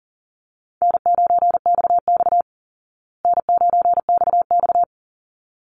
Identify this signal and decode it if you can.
Morse code, heard as a keyed tone, about 37 wpm, 700 Hz. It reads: N9XX N9XX
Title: Easy Call Signs Repeated 2x - 40wpm